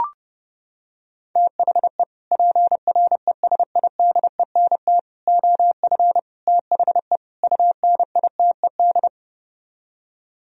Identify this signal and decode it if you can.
Morse code, heard as a keyed tone, about 32 wpm, 700 Hz. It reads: THE PRESIDENT OF THE UNITED